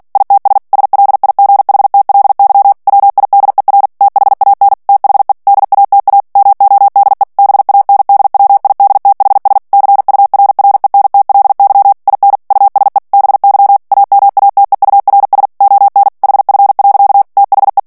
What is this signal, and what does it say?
Morse code, heard as a keyed tone, about 48 wpm, 800 Hz. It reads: ITS SLIGHTLY WIDER THAN THE DATA MODE BANDWIDTHS CURRENTLY IN USE BY AMATEURS ON HF. THE
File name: ANGLAIS texte clair 3 vitesse 45 mots minute